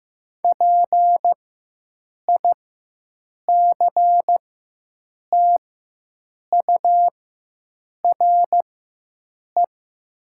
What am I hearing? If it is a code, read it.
Morse code, heard as a keyed tone, about 15 wpm, spaced as for 7 wpm, 700 Hz. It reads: PICTURE
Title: 4x Character Spacing - Top 200 Words - 15wpm